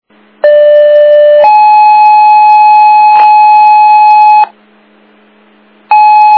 Two-tone page, heard as a one-second tone, then a three-second tone, about 590 then 850 Hz. It is heard from 0.4 to 4.4 s.